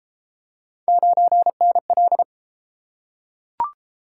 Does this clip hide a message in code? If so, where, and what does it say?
Yes: Morse code from 0.9 to 2.2 s, heard as a keyed tone, about 30 wpm, 700 Hz. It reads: 9NL